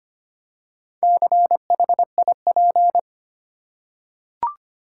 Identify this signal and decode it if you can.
Morse code, heard as a keyed tone, about 25 wpm, 700 Hz. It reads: CHIP
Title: Four Letter Words - 25wpm